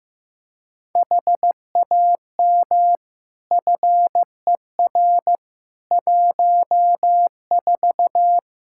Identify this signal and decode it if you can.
Morse code, heard as a keyed tone, about 15 wpm, 700 Hz. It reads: HAM FER 14
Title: Practice QSO - Exchange 3 - More Info - 15wpm